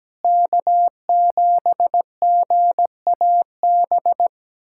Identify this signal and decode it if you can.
Morse code, heard as a keyed tone, about 17 wpm, 700 Hz. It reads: K7GAB
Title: USA Callsign DE Callsign - 5 Characters - 17wpm